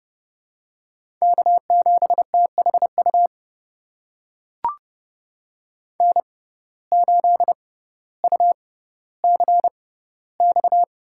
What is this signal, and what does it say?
Morse code, heard as a keyed tone, about 30 wpm, spaced as for 15 wpm, 700 Hz. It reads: K7THU N8UCX
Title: USA Call Signs - 5 Characters - 6x Character Spacing - 30wpm